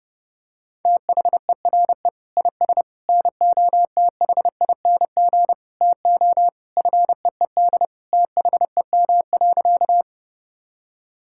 Morse code, heard as a keyed tone, about 30 wpm, 700 Hz. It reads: THERE IS NOTHING TO FEED THEM.
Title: Sentences from Top 1600 Words 30wpm